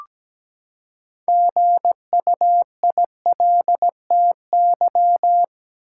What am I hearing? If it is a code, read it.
Morse code, heard as a keyed tone, about 17 wpm, 700 Hz. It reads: GUILTY
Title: Six Letter Words - 17wpm